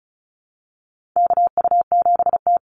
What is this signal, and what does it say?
Morse code, heard as a keyed tone, about 34 wpm, 700 Hz. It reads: KU7T
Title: Call Signs Repeated 3x - 35wpm